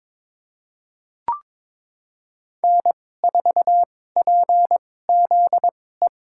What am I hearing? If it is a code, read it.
Morse code, heard as a keyed tone, about 23 wpm, spaced as for 16 wpm, 700 Hz. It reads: N4PZE
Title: USA Call Signs - 5 Characters - 2x Character Spacing - 22wpm